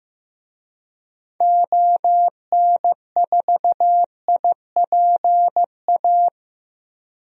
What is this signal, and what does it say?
Morse code, heard as a keyed tone, about 15 wpm, 700 Hz. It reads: ON4IPA
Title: DXCC-100 Call Signs - 6 Characters - Repeated 3x - 15wpm